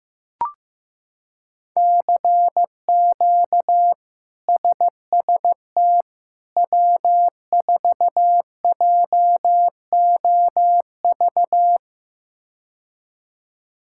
Morse code, heard as a keyed tone, about 15 wpm, 700 Hz. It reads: CQ SST W4JOV
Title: K1USN SST QSO Practice - 15wpm